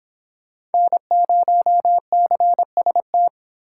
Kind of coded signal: Morse code

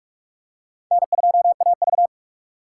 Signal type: Morse code

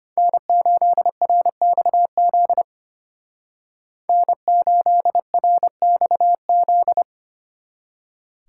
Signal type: Morse code

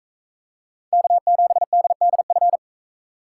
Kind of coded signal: Morse code

